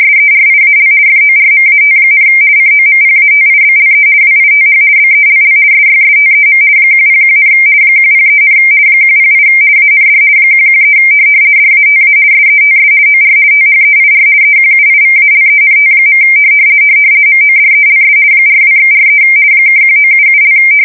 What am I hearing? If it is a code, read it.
RTTY, about 50 baud. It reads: SERIAL FORMAT, COMPATIBLE WITH BELL SYSTEM 101, 103, 108 AND 113 DATA SETS OR THEIR EQUIVALENT. MODELS 42/43 (1977)
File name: fsk170-50B / teletype-corp